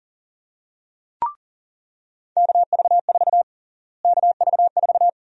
Morse code, heard as a keyed tone, about 40 wpm, 700 Hz. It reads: KV4 KV4
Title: USA Call Sign Prefixes - Repeated 2x - 40wpm